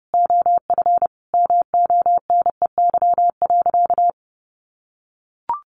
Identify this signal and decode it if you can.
Morse code, heard as a keyed tone, about 28 wpm, 700 Hz. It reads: OF MONEY.
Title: Sentences from Top 1300 Words 30wpm